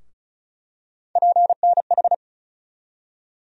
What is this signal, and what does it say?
Morse code, heard as a keyed tone, about 34 wpm, 700 Hz. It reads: PNH
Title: USA Call Sign Postfixes - 3 Letters 35wpm